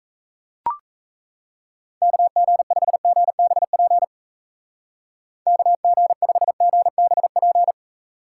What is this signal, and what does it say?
Morse code, heard as a keyed tone, about 34 wpm, 700 Hz. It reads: KG5GBP KG5GBP
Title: Speed Racing - USA 2x3 Call Signs 28wpm